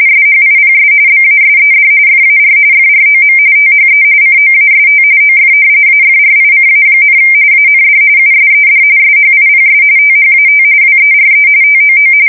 RTTY, about 45.45 baud. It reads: SHIPS BASED FAR FROM THE COAST OF AFRICA, AND USED SHIPS THAT WERE TOO
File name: fsk170 / usnavy2